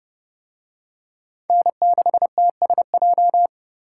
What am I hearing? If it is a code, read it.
Morse code, heard as a keyed tone, about 30 wpm, 700 Hz. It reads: N6TSJ